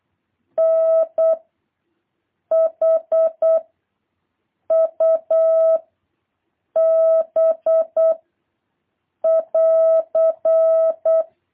Morse code, heard as a keyed tone, about 8 wpm, 650 Hz. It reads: N H U B <AR>
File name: jh8cbh / 2023cw